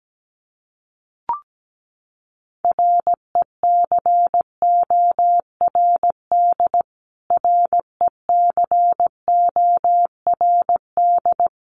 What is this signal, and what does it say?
Morse code, heard as a keyed tone, about 17 wpm, 700 Hz. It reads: RECORD RECORD
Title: Top 500 Words Repeated 17wpm